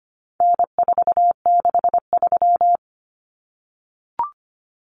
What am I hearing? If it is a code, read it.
Morse code, heard as a keyed tone, about 25 wpm, 700 Hz. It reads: N463
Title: Common Cut Numbers - 4-Digit 25wpm